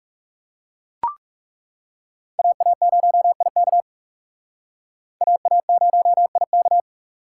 Morse code, heard as a keyed tone, about 39 wpm, 700 Hz. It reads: AA0IK AA0IK